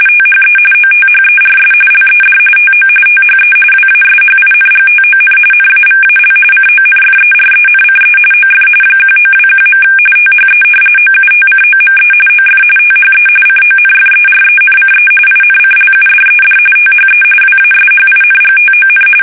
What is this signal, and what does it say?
RTTY, about 45.45 baud. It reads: THE OTHERS LED TO THE WASHINGTON NAVAL CONFERENCE OF 1921. THE OUTCOME OF THE CONFERENCE INCLUDED THE WASHINGTON
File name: fsk850